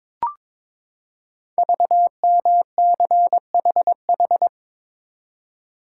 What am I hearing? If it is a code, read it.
Morse code, heard as a keyed tone, about 22 wpm, 700 Hz. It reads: VMCHH
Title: Code Groups - 5 Letter:Numbers - 22wpm